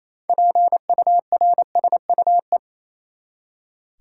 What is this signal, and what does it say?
Morse code, heard as a keyed tone, about 27 wpm, 700 Hz. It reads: PURSUE
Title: Six Letter Words - 28wpm